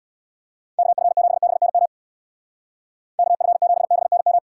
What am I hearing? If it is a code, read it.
Morse code, heard as a keyed tone, about 64 wpm, 700 Hz. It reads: KF7BNM KF7BNM